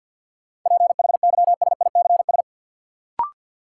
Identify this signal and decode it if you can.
Morse code, heard as a keyed tone, about 52 wpm, 700 Hz. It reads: PHYSICS
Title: Sentences from Top 1500 Words - New Word Drill Alt - 50wpm